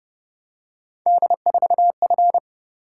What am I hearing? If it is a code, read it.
Morse code, heard as a keyed tone, about 26 wpm, 700 Hz. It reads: D4F